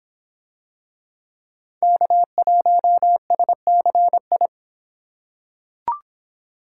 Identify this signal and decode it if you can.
Morse code, heard as a keyed tone, about 24 wpm, 700 Hz. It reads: K1SCI